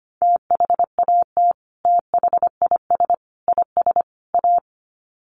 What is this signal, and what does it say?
Morse code, heard as a keyed tone, about 25 wpm, 700 Hz. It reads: THAT THIS IS A
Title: Sets of 4 Words 25wpm